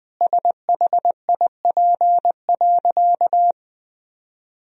Morse code, heard as a keyed tone, about 20 wpm, 700 Hz. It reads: SHIP.
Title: Sentences from Top 1000 Words - Review Part 2 - 20wpm